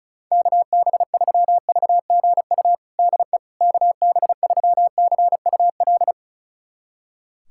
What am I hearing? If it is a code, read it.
Morse code, heard as a keyed tone, about 36 wpm, 700 Hz. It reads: KB3VGU DE KB3CUL
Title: USA Callsign DE Callsign 35wpm